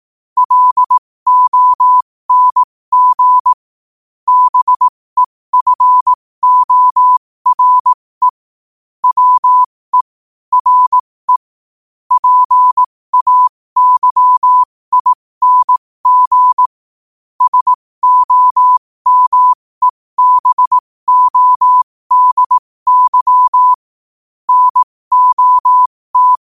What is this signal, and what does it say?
Morse code, heard as a keyed tone, about 18 wpm, spaced as for 15 wpm, 1000 Hz. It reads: LONG BEFORE WE RE PAYING SOMEBODY NOT
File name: qotdfiles / 15wpm